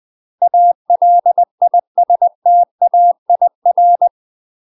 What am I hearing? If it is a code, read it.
Morse code, heard as a keyed tone, about 20 wpm, 700 Hz. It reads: ALISTAIR
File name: UK MASTER QSOs TRAINING / 20WPM / UK MASTER QSOs 0041